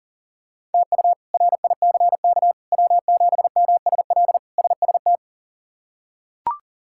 Morse code, heard as a keyed tone, about 40 wpm, 700 Hz. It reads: TU RICK W7MSL SST
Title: K1USN SST QSO Practice - 40wpm